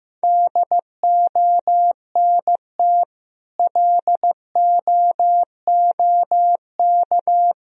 Morse code, heard as a keyed tone, about 15 wpm, 700 Hz. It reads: DONT LOOK
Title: Sentences from Top 1000 Words - Review Part 1 - 15wpm